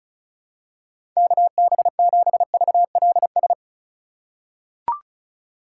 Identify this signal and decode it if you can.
Morse code, heard as a keyed tone, about 34 wpm, 700 Hz. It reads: KB7VLS
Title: USA Call Signs - 6 Characters - 3x Character Spacing - 35wpm